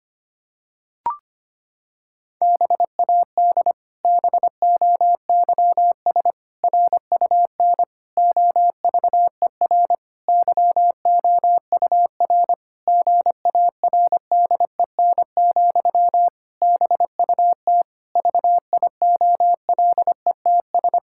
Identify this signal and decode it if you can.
Morse code, heard as a keyed tone, about 25 wpm, 700 Hz. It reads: BAD BOYS RUN OVER YOUR GARDEN, BUT VIOLETS
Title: Pangram Challenge v2 - 25wpm